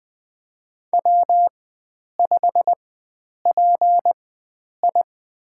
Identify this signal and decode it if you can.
Morse code, heard as a keyed tone, about 20 wpm, spaced as for 9 wpm, 700 Hz. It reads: W5PI
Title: USA Call Signs - 4 Characters - 4x Character Spacing - 20wpm